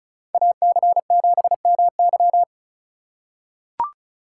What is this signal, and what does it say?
Morse code, heard as a keyed tone, about 34 wpm, 700 Hz. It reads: AC7MY
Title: USA Call Signs - 5 Characters - 3x Character Spacing - 35wpm